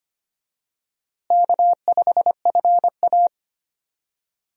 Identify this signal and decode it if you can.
Morse code, heard as a keyed tone, about 25 wpm, 700 Hz. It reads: K5FA
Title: USA Call Signs - 4 Characters - Repeated 2x - 25wpm